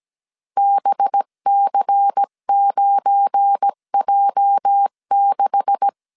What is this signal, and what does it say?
Morse code, heard as a keyed tone, about 17 wpm, 800 Hz. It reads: BC9J6
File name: 1500C S B 16wpm